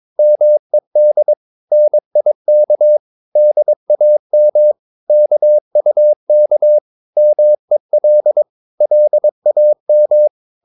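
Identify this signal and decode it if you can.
Morse code, heard as a keyed tone, about 22 wpm, 600 Hz. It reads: MED NIK DAM KUK MEL LAM
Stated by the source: Morse practice of short Danish words at 22 wpm, from CW med Gnister.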